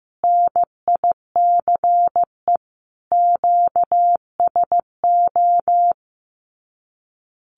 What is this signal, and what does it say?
Morse code, heard as a keyed tone, about 15 wpm, 700 Hz. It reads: NICE QSO
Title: Common QSO Phrases 15wpm